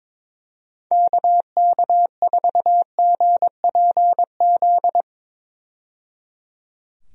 Morse code, heard as a keyed tone, about 22 wpm, 700 Hz. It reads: KK4GPZ
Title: USA Call Signs - 6 Characters - 22wpm